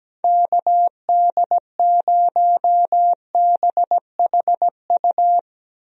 Morse code, heard as a keyed tone, about 17 wpm, 700 Hz. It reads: KD0BHU
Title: USA Call Signs - 6 Characters - Repeated 2x - 17wpm